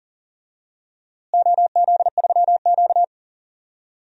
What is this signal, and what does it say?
Morse code, heard as a keyed tone, about 40 wpm, 700 Hz. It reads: OZ3Q
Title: DXCC-100 Call Signs - 4 Characters - Repeated 3x - 40wpm